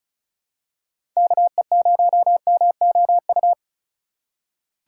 Morse code, heard as a keyed tone, about 34 wpm, 700 Hz. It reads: KE0MOU
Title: USA Call Signs - 6 Characters - 2x Character Spacing - 35wpm